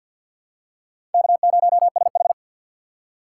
Morse code, heard as a keyed tone, about 53 wpm, 700 Hz. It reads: K0SH
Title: USA Call Signs - 4 Characters - 2x Character Spacing - 50wpm